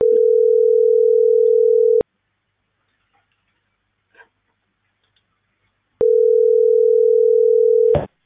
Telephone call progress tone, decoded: ringback tone

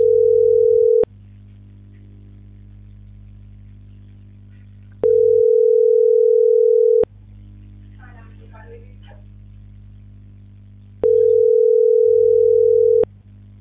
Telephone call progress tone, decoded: ringback tone

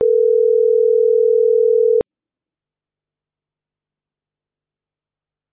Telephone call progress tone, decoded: ringback tone

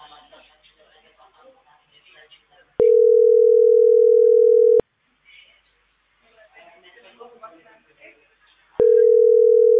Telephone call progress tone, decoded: ringback tone

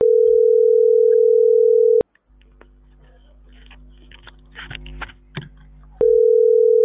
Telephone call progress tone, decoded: ringback tone

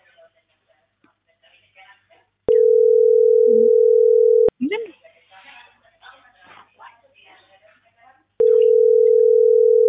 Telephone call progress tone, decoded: ringback tone

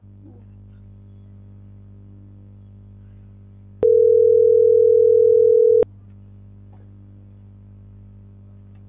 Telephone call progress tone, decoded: ringback tone

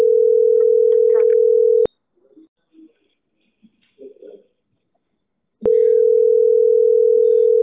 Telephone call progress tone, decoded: ringback tone

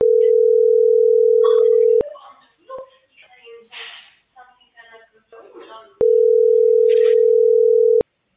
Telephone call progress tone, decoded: ringback tone

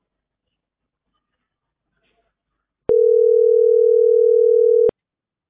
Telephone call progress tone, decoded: ringback tone

